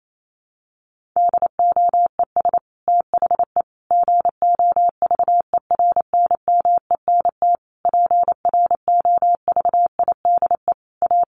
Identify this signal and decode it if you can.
Morse code, heard as a keyed tone, about 28 wpm, 700 Hz. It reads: DOES THE GOVERNMENT PROVIDE A
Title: Sentences from Top 1500 Words 28wpm